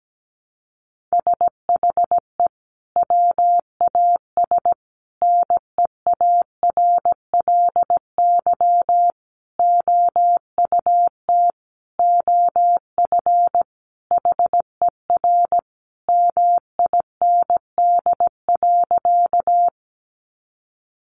Morse code, heard as a keyed tone, about 17 wpm, 700 Hz. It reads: SHE WAS NEARLY OUT OF HER MIND.
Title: Sentences from Top 1300 Words 17wpm